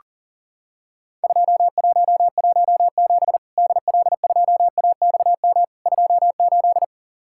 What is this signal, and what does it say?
Morse code, heard as a keyed tone, about 40 wpm, 700 Hz. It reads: 2117 DL2AXM 28
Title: Worked All Europe DX Contest - QTC Exchange - 40wpm